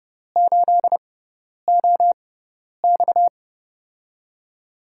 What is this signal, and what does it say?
Morse code, heard as a keyed tone, about 30 wpm, spaced as for 10 wpm, 700 Hz. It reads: 8OX
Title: USA Call Signs - 6x Character Spacing - 30wpm